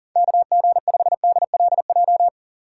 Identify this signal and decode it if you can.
Morse code, heard as a keyed tone, about 40 wpm, 700 Hz. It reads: KG5DLJ
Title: USA Call Signs - 2x Character Spacing - 40wpm